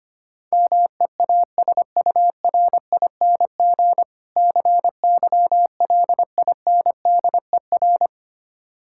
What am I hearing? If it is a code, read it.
Morse code, heard as a keyed tone, about 26 wpm, 700 Hz. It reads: MEASURING CYLINDER
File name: practice / 25wpm / Difficult Words - Speed-Racing with Hints - 25wpm